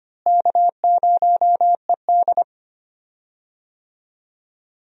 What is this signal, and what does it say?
Morse code, heard as a keyed tone, about 25 wpm, 700 Hz. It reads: K0ED
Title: USA Call Signs - 4 Characters - Repeated 3x - 25wpm